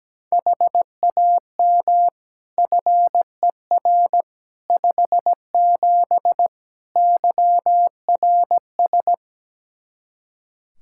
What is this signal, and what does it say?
Morse code, heard as a keyed tone, about 17 wpm, 700 Hz. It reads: HAM FER 57 YRS
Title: Practice QSO - Exchange 3 - More Info - 17wpm